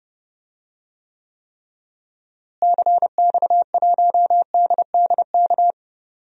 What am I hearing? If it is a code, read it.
Morse code, heard as a keyed tone, about 30 wpm, 700 Hz. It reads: CX1DDK
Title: DXCC-100 Call Signs - 6 Characters - Repeated 3x - 30wpm